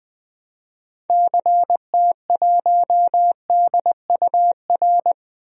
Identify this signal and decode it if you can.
Morse code, heard as a keyed tone, about 20 wpm, 700 Hz. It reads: CT1DUR
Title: DXCC-100 Call Signs - 6 Characters - Repeated 3x - 20wpm